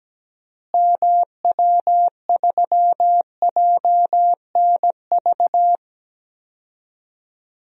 Morse code, heard as a keyed tone, about 17 wpm, 700 Hz. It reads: MW3JNV
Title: DXCC-100 Call Signs - 6 Characters - Repeated 3x - 17wpm